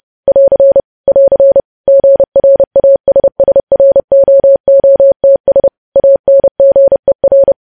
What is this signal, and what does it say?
Morse code, heard as a keyed tone, about 30 wpm, 550 Hz. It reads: <AR> <AR> GRASSROOTS ANGER
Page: news headlines in morse code at 30 wpm